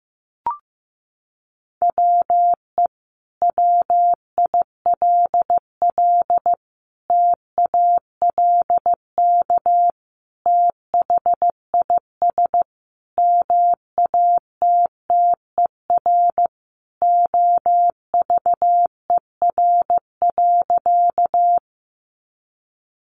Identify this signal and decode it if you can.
Morse code, heard as a keyed tone, about 15 wpm, 700 Hz. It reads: WE WILL TALK THIS MATTER OVER.
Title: Sentences from Top 700 Words 15wpm